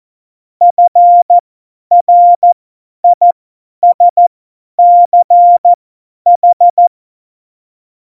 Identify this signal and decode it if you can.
Morse code, heard as a keyed tone, about 14 wpm, spaced as for 10 wpm, 700 Hz. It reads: FRISCH
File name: Morsekurs / Witze / Witz 20. August 2019, 50 F70